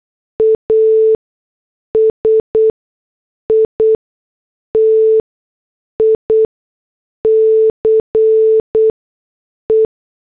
Morse code, heard as a keyed tone, about 8 wpm, spaced as for 6 wpm, 440 Hz. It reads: ASITICE